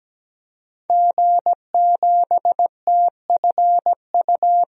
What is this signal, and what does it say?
Morse code, heard as a keyed tone, about 17 wpm, 700 Hz. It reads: G7TFU
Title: DXCC-100 Call Signs - 5 Characters - 17wpm